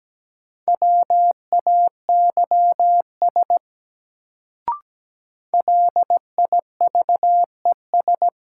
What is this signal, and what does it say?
Morse code, heard as a keyed tone, about 17 wpm, 700 Hz. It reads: WAYS LIVES
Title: Sets of 5 Words Encore - New Word Drill - 17wpm